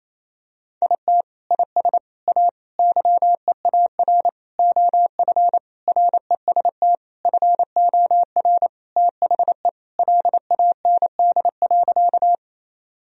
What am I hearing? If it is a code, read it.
Morse code, heard as a keyed tone, about 28 wpm, 700 Hz. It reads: IT IS A YEAR OF REST FOR THE LAND.
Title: Sentences from Top 1000 Words - Review Part 1 - 28wpm